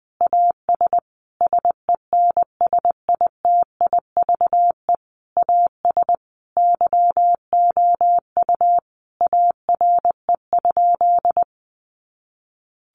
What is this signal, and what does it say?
Morse code, heard as a keyed tone, about 20 wpm, 700 Hz. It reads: AS SENSITIVE AS YOU ARE?
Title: Sentences from Top 1400 Words 20wpm